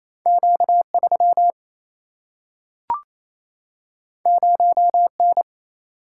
Morse code, heard as a keyed tone, about 28 wpm, 700 Hz. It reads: Q3 0N
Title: Single-Letter Single-Number 28wpm